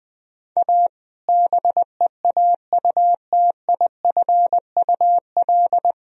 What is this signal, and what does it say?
Morse code, heard as a keyed tone, about 21 wpm, 700 Hz. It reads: A BEAUTIFUL
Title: Sentences from Top 1000 Words - Review Part 1 - 20wpm